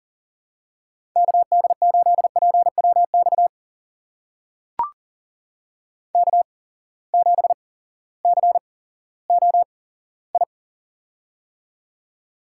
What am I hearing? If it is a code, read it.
Morse code, heard as a keyed tone, about 38 wpm, spaced as for 18 wpm, 700 Hz. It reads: KD8PWX K7COI
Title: USA Call Signs - 8x Character Spacing - 40wpm